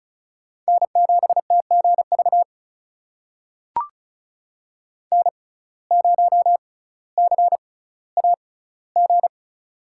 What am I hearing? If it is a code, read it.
Morse code, heard as a keyed tone, about 36 wpm, spaced as for 18 wpm, 700 Hz. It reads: N7TGV N0CAG
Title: USA Call Signs - 5 Characters - 6x Character Spacing - 35wpm